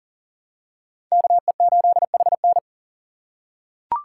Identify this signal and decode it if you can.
Morse code, heard as a keyed tone, about 40 wpm, 700 Hz. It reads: KE8HN
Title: USA Call Signs - 5 Characters - 6x Character Spacing - 40wpm